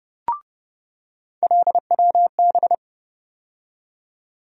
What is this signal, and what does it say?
Morse code, heard as a keyed tone, about 30 wpm, 700 Hz. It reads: LWB
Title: USA Call Sign Postfixes - 3 Letters 30wpm